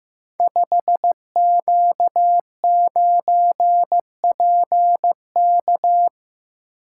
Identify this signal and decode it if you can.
Morse code, heard as a keyed tone, about 15 wpm, 700 Hz. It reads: 5Q9PK
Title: Code Groups - 5 Letter:Numbers - 15wpm